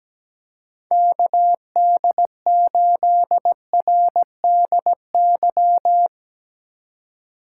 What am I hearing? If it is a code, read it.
Morse code, heard as a keyed tone, about 17 wpm, 700 Hz. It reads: KD8RDY
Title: USA Call Signs - 6 Characters - 6x Character Spacing - 17wpm